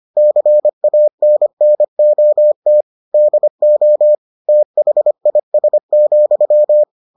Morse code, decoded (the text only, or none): CANNOT DO THIS,